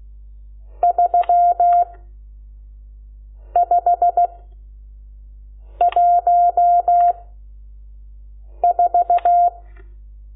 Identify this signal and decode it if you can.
Morse code, heard as a keyed tone, about 16 wpm, 700 Hz. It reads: 3 5 1 4
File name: Lecciones / CLASE Nº12